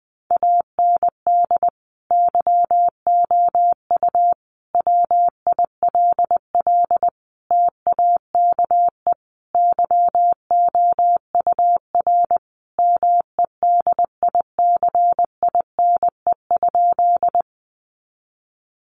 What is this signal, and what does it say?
Morse code, heard as a keyed tone, about 20 wpm, 700 Hz. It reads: AND YOU WILL TAKE YOUR MEDICINE?
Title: Sentences from Top 900 Words 20wpm